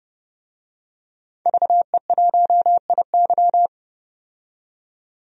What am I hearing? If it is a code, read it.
Morse code, heard as a keyed tone, about 30 wpm, 700 Hz. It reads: VE1IY
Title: DXCC-100 Call Signs - 5 Characters - 30wpm